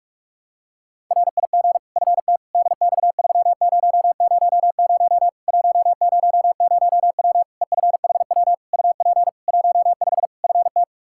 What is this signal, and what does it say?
Morse code, heard as a keyed tone, about 46 wpm, 700 Hz. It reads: RIG FT DX3000 100W EFHW UP 15 FT
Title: Practice QSO - Exchange 2 - Info - 45wpm